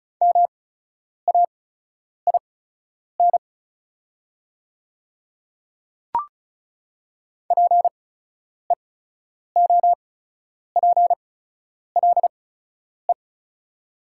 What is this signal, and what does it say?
Morse code, heard as a keyed tone, about 34 wpm, spaced as for 10 wpm, 700 Hz. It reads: MAIN PEOPLE
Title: 8x Character Spacing - Top 300 Words - 35wpm